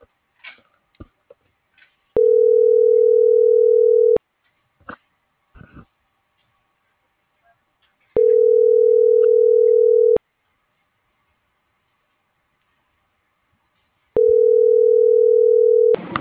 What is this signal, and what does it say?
Ringback tone (North America), 440 + 480 Hz.